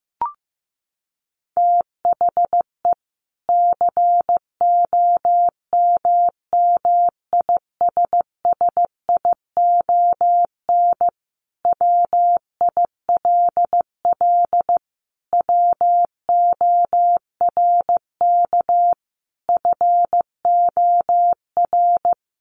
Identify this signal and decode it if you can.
Morse code, heard as a keyed tone, about 15 wpm, 700 Hz. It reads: THE COMMISSION WILL WORK FOR
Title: Sentences from Top 1400 Words 15wpm